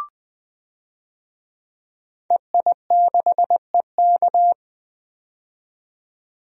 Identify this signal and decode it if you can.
Morse code, heard as a keyed tone, about 20 wpm, 700 Hz. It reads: EI6EK
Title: DXCC-100 Call Signs - 5 Characters - 20wpm